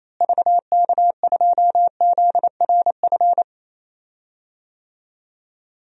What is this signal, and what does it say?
Morse code, heard as a keyed tone, about 28 wpm, 700 Hz. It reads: VK2ZRF
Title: DXCC-100 Call Signs - 6 Characters - Repeated 2x - 28wpm